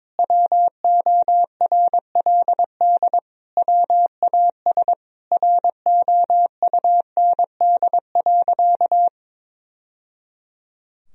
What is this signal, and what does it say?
Morse code, heard as a keyed tone, about 21 wpm, 700 Hz. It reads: WORLD WAS ROUND.
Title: Sentences from Top 1000 Words - Review Part 1 - 22wpm